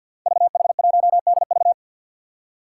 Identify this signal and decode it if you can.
Morse code, heard as a keyed tone, about 52 wpm, 700 Hz. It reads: VH1DV
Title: Five Challenging Letter-Number Combinations 50wpm